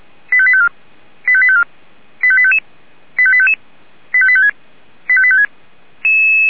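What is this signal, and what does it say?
Five-tone selective call (ZVEI), heard as quick runs of five tones, tones of about 70 ms. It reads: 85854 85854 85855 85855 85856 85856
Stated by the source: ZVEI five-tone selective call 85737, received on an SDR.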